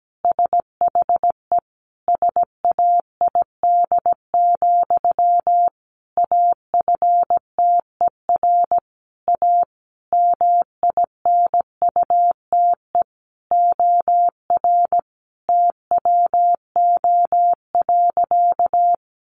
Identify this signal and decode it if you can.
Morse code, heard as a keyed tone, about 17 wpm, 700 Hz. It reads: SHE SAID, AFTER A MINUTE OR TWO.
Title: Sentences from Top 1400 Words 17wpm